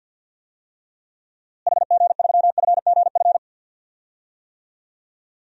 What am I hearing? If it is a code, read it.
Morse code, heard as a keyed tone, about 49 wpm, 700 Hz. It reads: HG3FGF
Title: DXCC-100 Call Signs - 6 Characters - Repeated 3x - 50wpm